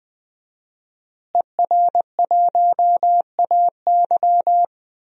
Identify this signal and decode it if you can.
Morse code, heard as a keyed tone, about 21 wpm, 700 Hz. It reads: ER1AY
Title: DXCC-100 Call Signs - 5 Characters - Repeated 2x - 20wpm